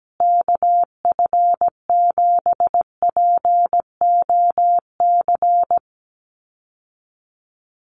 Morse code, heard as a keyed tone, about 17 wpm, 700 Hz. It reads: KF7POC
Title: USA Call Signs Repeated 2x 17wpm